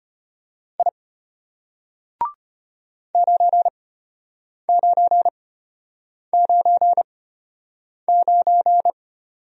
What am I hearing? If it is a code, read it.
Morse code, heard as a keyed tone, about 34 wpm, spaced as for 8 wpm, 700 Hz. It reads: I 9999